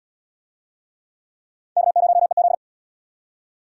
Morse code, heard as a keyed tone, about 76 wpm, 700 Hz. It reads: K0EZ